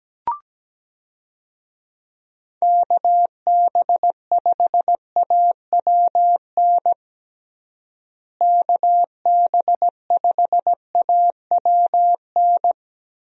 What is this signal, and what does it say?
Morse code, heard as a keyed tone, about 17 wpm, 700 Hz. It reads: KB5AWN KB5AWN
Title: USA Call Signs - 6 Characters - Repeated 3x - 17wpm